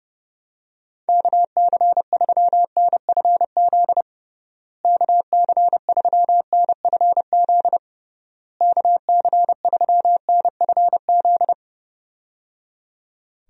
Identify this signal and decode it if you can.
Morse code, heard as a keyed tone, about 30 wpm, 700 Hz. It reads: KC3NFZ KC3NFZ KC3NFZ
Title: USA Call Signs - 6 Characters - Repeated 3x - 30wpm